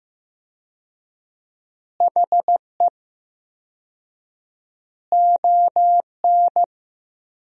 Morse code, heard as a keyed tone, about 15 wpm, 700 Hz. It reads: HE ON
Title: 4x Word-Spacing - Sentences from Top 500 Words - 15wpm